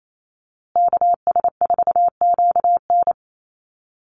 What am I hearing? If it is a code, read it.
Morse code, heard as a keyed tone, about 28 wpm, 700 Hz. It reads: KS4QN
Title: USA Call Signs Repeated 3x 28wpm